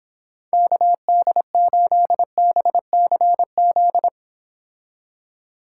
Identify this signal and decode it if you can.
Morse code, heard as a keyed tone, about 26 wpm, 700 Hz. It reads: KD8BCZ